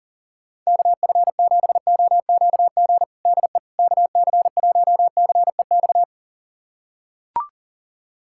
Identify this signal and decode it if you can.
Morse code, heard as a keyed tone, about 37 wpm, 700 Hz. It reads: KF7OQG DE KC1CEX
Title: USA Callsign DE Callsign - 6 Characters - 40wpm